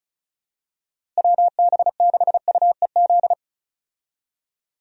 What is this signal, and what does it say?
Morse code, heard as a keyed tone, about 35 wpm, 700 Hz. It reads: WB6UEZ
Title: USA Call Signs - 6 Characters - Repeated 2x - 35wpm